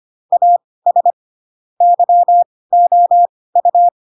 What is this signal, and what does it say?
Morse code, heard as a keyed tone, about 25 wpm, spaced as for 18 wpm, 700 Hz. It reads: AS YOU
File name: Pinocchio / Pinocchio ch4 12 25 ik1hge